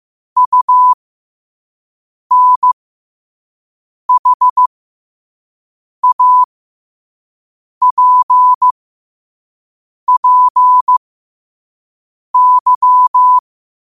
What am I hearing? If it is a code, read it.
Morse code, heard as a keyed tone, about 15 wpm, spaced as for 5 wpm, 1000 Hz. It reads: UNHAPPY
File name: qotdfiles / 5wpm